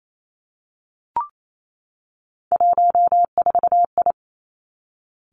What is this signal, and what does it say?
Morse code, heard as a keyed tone, about 28 wpm, 700 Hz. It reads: 14I
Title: Three Challenging Letter-Number Combinations 28wpm